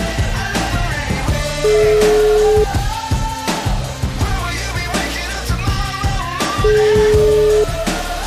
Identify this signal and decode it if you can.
Ringback tone (Europe, 425 Hz), with noise.